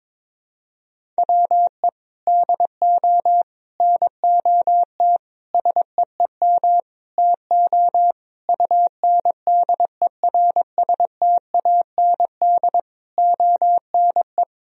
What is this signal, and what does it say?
Morse code, heard as a keyed tone, about 22 wpm, 700 Hz. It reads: WE DO NOT SEEM TO UNDERSTAND ONE
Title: Sentences from Top 1000 Words - Review Part 1 - 22wpm